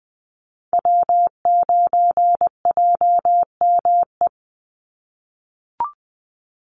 Morse code, heard as a keyed tone, about 20 wpm, 700 Hz. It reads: W9JME
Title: USA Call Signs 20wpm